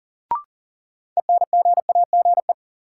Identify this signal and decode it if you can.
Morse code, heard as a keyed tone, about 40 wpm, 700 Hz. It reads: ENGAGE
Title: Sentences from Top 1500 Words - New Word Drill Alt - 40wpm